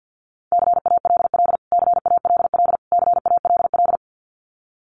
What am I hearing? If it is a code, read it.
Morse code, heard as a keyed tone, about 49 wpm, 700 Hz. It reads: CALL CALL CALL
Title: Top 300 Words Repeated 50wpm